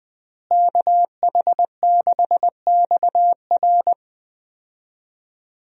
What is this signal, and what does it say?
Morse code, heard as a keyed tone, about 20 wpm, 700 Hz. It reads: KH6XR
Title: DXCC-100 Call Signs - 5 Characters - Repeated 3x - 20wpm